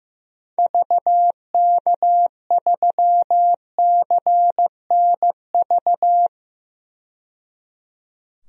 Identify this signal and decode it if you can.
Morse code, heard as a keyed tone, about 15 wpm, 700 Hz. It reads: VK3CNV
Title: DXCC-100 Call Signs - 6 Characters - 15wpm